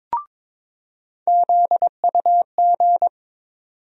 Morse code, heard as a keyed tone, about 22 wpm, 700 Hz. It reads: ZUG
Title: USA Call Sign Postfixes - 3 Letters 22wpm